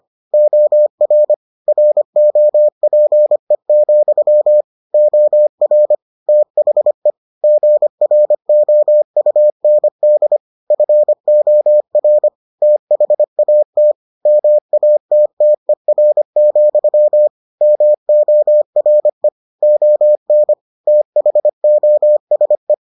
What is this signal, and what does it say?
Morse code, heard as a keyed tone, about 25 wpm, 600 Hz. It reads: OR ROPE, OR THE GROUND FOR THAT MATTER, MORE ON THOSE